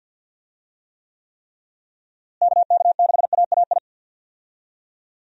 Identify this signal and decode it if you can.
Morse code, heard as a keyed tone, about 53 wpm, 700 Hz. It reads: KK6AAI
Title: USA Call Signs - 6 Characters - Repeated 3x - 50wpm